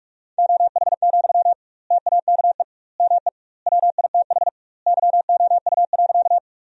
Morse code, heard as a keyed tone, about 43 wpm, 700 Hz. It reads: OH, TAKE ME WITH YOU.
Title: 2x Word-Spacing - Sentences from Top 500 Words - 45wpm